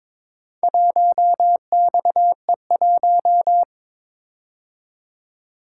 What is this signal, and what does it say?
Morse code, heard as a keyed tone, about 22 wpm, 700 Hz. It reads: 1XE1
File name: practice / 22wpm / Four Letter-Number Combinations - 22wpm